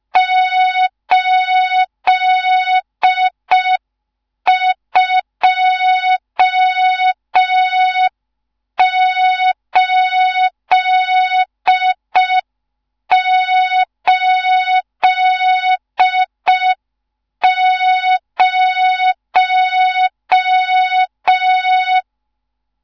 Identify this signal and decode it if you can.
Morse code, heard as a keyed tone, about 5 wpm, 3700 Hz, with noise. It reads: 82880
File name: Morze-HA2MN / hang / 14 280 HA2MN